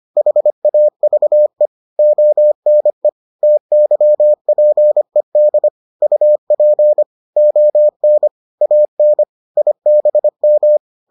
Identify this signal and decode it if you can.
Morse code, heard as a keyed tone, about 25 wpm, 600 Hz. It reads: HAVE ONE TYPED UP ON AN IBM